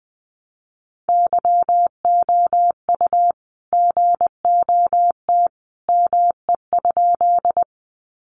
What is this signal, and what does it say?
Morse code, heard as a keyed tone, about 20 wpm, 700 Hz. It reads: YOU GOT ME?
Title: Sentences from Top 300 Words 20wpm